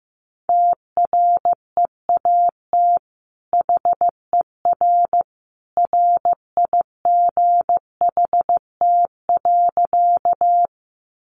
Morse code, heard as a keyed tone, about 15 wpm, 700 Hz. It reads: TREAT HER RIGHT.
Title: Sentences from Top 1300 Words 15wpm